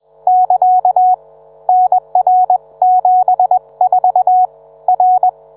Morse code, heard as a keyed tone, about 21 wpm, 740 Hz. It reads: <KA> NR74 R